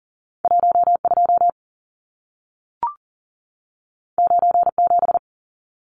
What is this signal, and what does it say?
Morse code, heard as a keyed tone, about 40 wpm, 700 Hz. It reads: 12 97
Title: Numbers - 2-Digit 40wpm